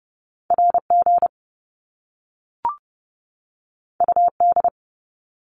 Morse code, heard as a keyed tone, about 30 wpm, 700 Hz. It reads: RG UD
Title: Top Two Letter Combinations in Words 30wpm